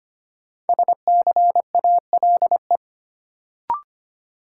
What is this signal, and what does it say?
Morse code, heard as a keyed tone, about 25 wpm, 700 Hz. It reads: SCALE
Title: Sentences from Top 1000 Words - Review Part 4 - New Word Drill - 25wpm